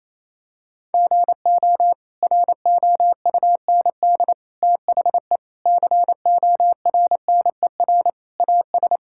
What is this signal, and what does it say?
Morse code, heard as a keyed tone, about 28 wpm, 700 Hz. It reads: GO ROUND THE CORNER AS
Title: Sentences from Top 1600 Words 28wpm